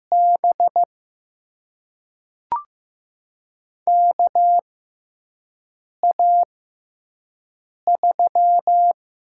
Morse code, heard as a keyed tone, about 15 wpm, spaced as for 5 wpm, 700 Hz. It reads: B KA3
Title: USA Call Signs - 6 Characters - 6x Character Spacing - 15wpm